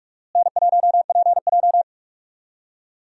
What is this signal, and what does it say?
Morse code, heard as a keyed tone, about 44 wpm, 700 Hz. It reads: N1PJ
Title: USA Call Signs - 4 Characters - Repeated 2x - 45wpm